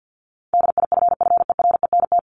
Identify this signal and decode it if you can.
Morse code, heard as a keyed tone, about 50 wpm, 700 Hz. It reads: DIFFERENT
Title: Top 501-750 Words 50wpm